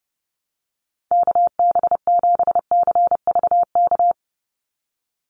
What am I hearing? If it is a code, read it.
Morse code, heard as a keyed tone, about 30 wpm, 700 Hz. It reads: KB7CVK
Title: USA Call Signs 30wpm